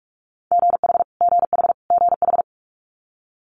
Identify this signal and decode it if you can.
Morse code, heard as a keyed tone, about 46 wpm, 700 Hz. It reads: GH GH GH
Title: Multi-letter Phonemes Repeated 3x 45wpm